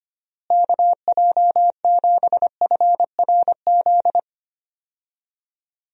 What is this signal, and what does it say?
Morse code, heard as a keyed tone, about 25 wpm, 700 Hz. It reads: KJ7FRZ
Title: USA Call Signs - 6 Characters - Repeated 3x - 25wpm